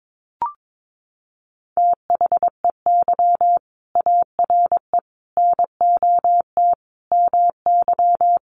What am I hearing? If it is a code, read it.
Morse code, heard as a keyed tone, about 22 wpm, 700 Hz. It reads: THEY ARE NOT MY
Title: Sentences from Top 1100 Words 22wpm